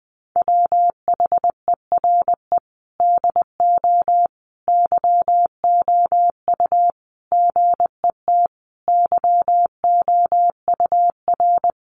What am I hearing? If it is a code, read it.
Morse code, heard as a keyed tone, about 20 wpm, 700 Hz. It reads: WHERE DO YOU GET YOUR
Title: Sentences from Top 600 Words 20wpm